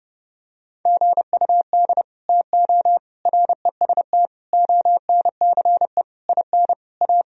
Morse code, heard as a keyed tone, about 30 wpm, 700 Hz. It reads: GUD TO REST ONCE IN A
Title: Sentences from Top 500 Words and Abbreviations 30wpm